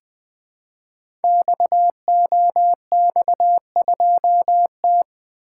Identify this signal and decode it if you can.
Morse code, heard as a keyed tone, about 20 wpm, 700 Hz. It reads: XOX2T
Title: Code Groups - 5 Letter:Numbers - 20wpm